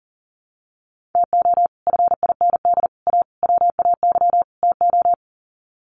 Morse code, heard as a keyed tone, about 40 wpm, 700 Hz. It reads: TO FIND A WAY TO
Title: Sets of 5 Words 40wpm